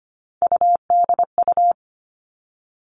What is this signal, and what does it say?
Morse code, heard as a keyed tone, about 25 wpm, 700 Hz. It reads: UDU
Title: Three Challenging Letter-Number Combinations 25wpm